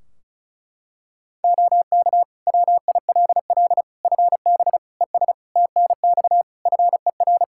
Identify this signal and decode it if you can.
Morse code, heard as a keyed tone, about 34 wpm, 700 Hz. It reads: OK WILL FB ES TNX FER
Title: Practice QSO - Exchange 2 - Info - 35wpm